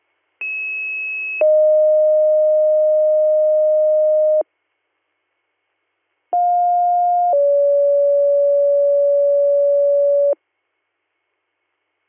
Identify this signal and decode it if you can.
Two-tone page, heard as a one-second tone, then a three-second tone, about 2600 then 620 Hz, about 710 then 570 Hz. It is heard from 0.4 to 10.3 s.